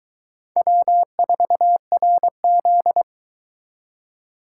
Morse code, heard as a keyed tone, about 23 wpm, 700 Hz. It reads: W4RZ